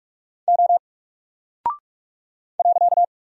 Morse code, heard as a keyed tone, about 44 wpm, 700 Hz. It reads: O .
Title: Single Character (Letter:Number:Punctuation:Prosign) Alt 45wpm